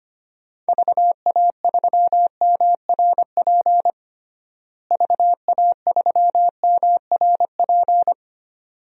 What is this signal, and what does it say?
Morse code, heard as a keyed tone, about 25 wpm, 700 Hz. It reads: VA3MRP VA3MRP
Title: DXCC-100 Call Signs - 6 Characters - Repeated 2x - 25wpm